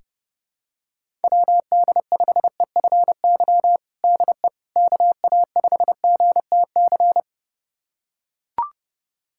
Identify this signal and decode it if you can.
Morse code, heard as a keyed tone, about 30 wpm, 690 Hz. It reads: WD5EFY DE KA5GTC
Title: USA Callsign DE Callsign - 6 Characters - 30wpm